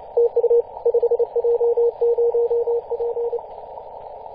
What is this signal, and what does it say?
Morse code, heard as a keyed tone, about 29 wpm, 470 Hz. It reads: TU 5J0P
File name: posts / 2016-mar-18-2347z-5j0p-21-025mhz-cw